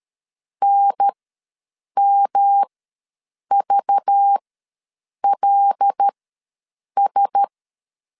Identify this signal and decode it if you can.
Morse code, heard as a keyed tone, about 13 wpm, spaced as for 7 wpm, 800 Hz. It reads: NMVLS
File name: LEZIONI / 5 LBVC / 5 LBV ws 12wpm 3015